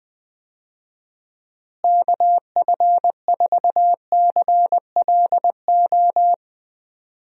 Morse code, heard as a keyed tone, about 20 wpm, 700 Hz. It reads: KF4CLO
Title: USA Call Signs - 6 Characters - Repeated 3x - 20wpm